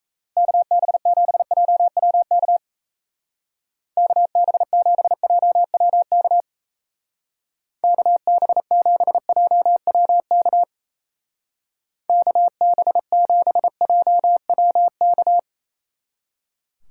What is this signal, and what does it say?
Morse code, heard as a keyed tone, about 36 wpm, 700 Hz. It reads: KB7JWK KB7JWK KB7JWK KB7JWK